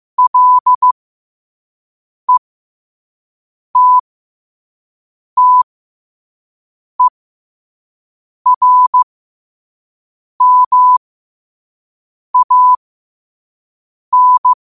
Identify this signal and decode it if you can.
Morse code, heard as a keyed tone, about 15 wpm, spaced as for 5 wpm, 1000 Hz. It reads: LETTERMAN
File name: qotdfiles / 5wpm